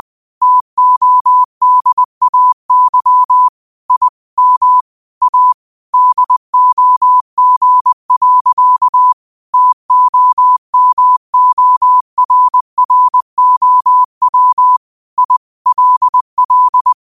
Morse code, heard as a keyed tone, about 20 wpm, 1000 Hz. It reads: TODAY I M A DOG. TOMORROW I LL
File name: qotdfiles / 20wpm